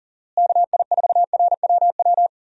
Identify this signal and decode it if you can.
Morse code, heard as a keyed tone, about 40 wpm, 700 Hz. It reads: KI4RWW
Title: USA Call Signs - 6 Characters - Repeated 2x - 40wpm